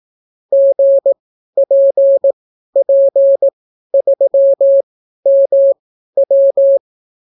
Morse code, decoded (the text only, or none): GPP3MW